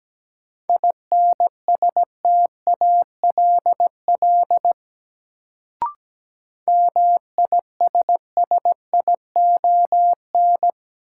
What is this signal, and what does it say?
Morse code, heard as a keyed tone, about 17 wpm, 700 Hz. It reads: INSTALL MISSION
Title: Sentences from Top 1600 Words - New Word Drill Alt - 17wpm